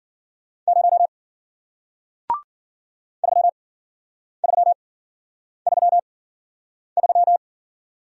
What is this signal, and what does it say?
Morse code, heard as a keyed tone, about 54 wpm, spaced as for 9 wpm, 700 Hz. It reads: 0 3333